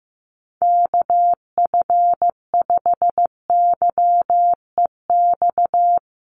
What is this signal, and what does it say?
Morse code, heard as a keyed tone, about 15 wpm, 700 Hz. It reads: KF5YEX
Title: USA Call Signs Repeated 3x 15wpm